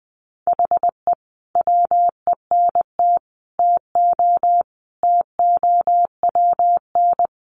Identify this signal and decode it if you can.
Morse code, heard as a keyed tone, about 20 wpm, 700 Hz. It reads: HE WENT TO TOWN
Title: Sentences from Top 1300 Words 20wpm